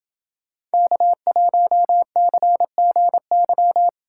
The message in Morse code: K1CGY

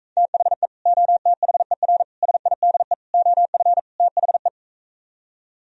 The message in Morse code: THE OTHER SIDE OF THE

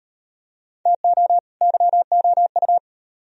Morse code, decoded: TO YOU

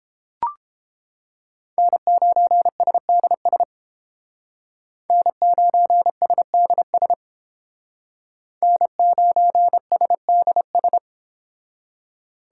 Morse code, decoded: N9SDS N9SDS N9SDS

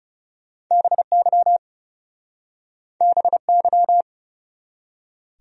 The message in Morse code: BY BY

